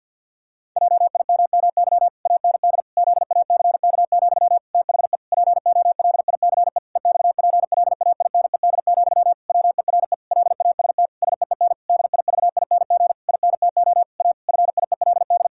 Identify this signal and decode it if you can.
Morse code, heard as a keyed tone, about 52 wpm, 700 Hz. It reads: JIMMY AND ZACK, THE POLICE EXPLAINED, WERE LAST SEEN DIVING INTO A FIELD
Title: Pangram Challenge v2 - 50wpm